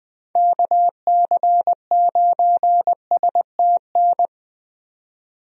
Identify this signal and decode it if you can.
Morse code, heard as a keyed tone, about 20 wpm, 700 Hz. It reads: KC9STN
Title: USA Call Signs - 6 Characters - 20wpm